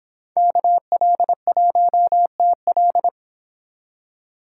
Morse code, heard as a keyed tone, about 26 wpm, 700 Hz. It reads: KL1TL